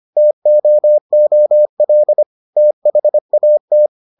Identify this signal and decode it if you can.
Morse code, heard as a keyed tone, about 25 wpm, 600 Hz. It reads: TOOL THAT